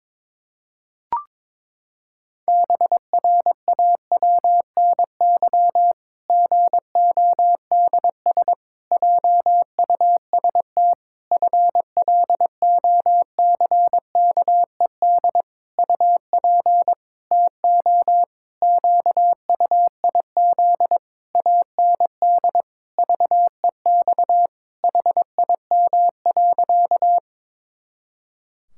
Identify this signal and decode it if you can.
Morse code, heard as a keyed tone, about 22 wpm, 700 Hz. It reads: BRAWNY GODS JUST FLOCKED UP TO QUIZ AND VEX HIM.
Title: Pangram Challenge v2 - 22wpm